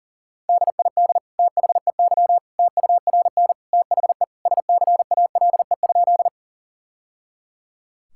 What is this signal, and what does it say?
Morse code, heard as a keyed tone, about 40 wpm, 700 Hz. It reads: DID THEY TURN THE SCALE?
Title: Sentences from Top 1000 Words - Review Part 5 - 40wpm